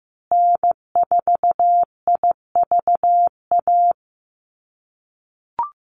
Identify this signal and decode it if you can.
Morse code, heard as a keyed tone, about 15 wpm, 700 Hz. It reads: N4IVA
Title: USA Call Signs Repeated 2x 15wpm